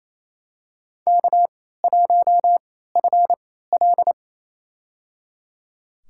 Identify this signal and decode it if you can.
Morse code, heard as a keyed tone, about 28 wpm, spaced as for 16 wpm, 700 Hz. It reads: K1FL
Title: USA Call Signs - 4 Characters - 3x Character Spacing - 28wpm